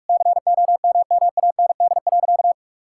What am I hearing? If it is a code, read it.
Morse code, heard as a keyed tone, about 46 wpm, 700 Hz. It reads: COMMAND.
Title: Sentences from Top 1600 Words 45wpm